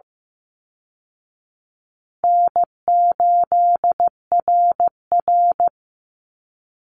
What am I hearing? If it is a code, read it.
Morse code, heard as a keyed tone, about 15 wpm, 700 Hz. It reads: N8RR
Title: Call Signs Repeated 2x - 15wpm